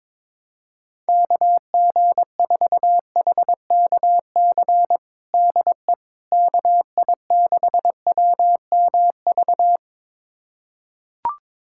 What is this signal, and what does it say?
Morse code, heard as a keyed tone, about 22 wpm, 700 Hz. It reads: KG4HKC DE KI6WMV
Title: USA Callsign DE Callsign 22wpm